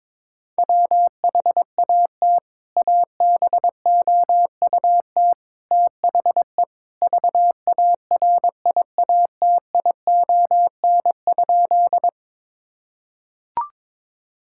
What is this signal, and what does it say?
Morse code, heard as a keyed tone, about 21 wpm, 700 Hz. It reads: WHAT ABOUT THE VARIATION?
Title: Sentences from Top 1600 Words 22wpm